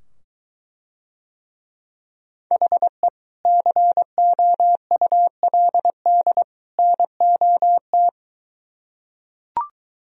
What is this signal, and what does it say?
Morse code, heard as a keyed tone, about 23 wpm, 700 Hz. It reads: HE COULD NOT